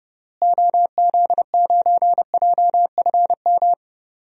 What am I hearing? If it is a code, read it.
Morse code, heard as a keyed tone, about 31 wpm, 700 Hz. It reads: OZ9JFM
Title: DXCC-100 Call Signs - 6 Characters - Repeated 2x - 30wpm